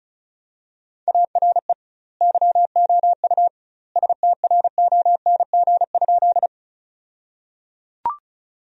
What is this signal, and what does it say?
Morse code, heard as a keyed tone, about 34 wpm, 700 Hz. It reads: ARE YOU STRONG?
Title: 2x Word-Spacing - Sentences from Top 500 Words - 35wpm